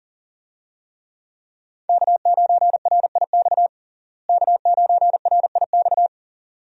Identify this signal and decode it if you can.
Morse code, heard as a keyed tone, about 40 wpm, 700 Hz. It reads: K9RIX K9RIX
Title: USA Call Signs - 5 Characters - Repeated 2x - 40wpm